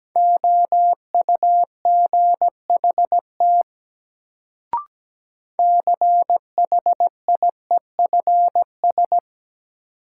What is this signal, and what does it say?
Morse code, heard as a keyed tone, about 17 wpm, 700 Hz. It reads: OUGHT CHIEFS
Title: Sets of 5 Words Encore - New Word Drill - 17wpm